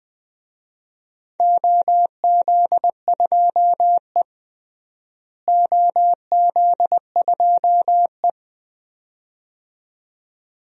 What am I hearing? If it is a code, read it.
Morse code, heard as a keyed tone, about 20 wpm, 700 Hz. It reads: OZ2E OZ2E
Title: DXCC-100 Call Signs - 4 Characters - Repeated 2x - 20wpm